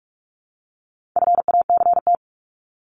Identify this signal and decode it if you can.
Morse code, heard as a keyed tone, about 44 wpm, 700 Hz. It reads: FACT
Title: Top 500 Words 45wpm